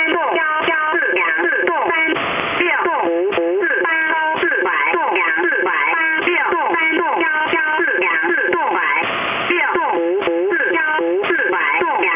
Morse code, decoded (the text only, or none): none